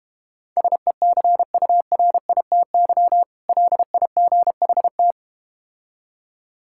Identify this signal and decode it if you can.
Morse code, heard as a keyed tone, about 31 wpm, 700 Hz. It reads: SECURITY LIGHT
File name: practice / 25wpm / Difficult Words - Speed-Racing with Hints - 25wpm